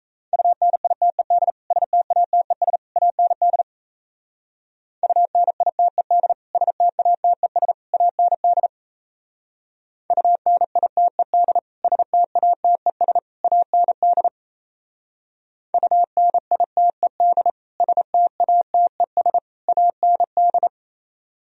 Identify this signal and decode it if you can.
Morse code, heard as a keyed tone, about 34 wpm, 700 Hz. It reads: UNITED STATES AND UNITED STATES AND UNITED STATES AND UNITED STATES AND
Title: Speed-Racing - Sets of 3 Words 28wpm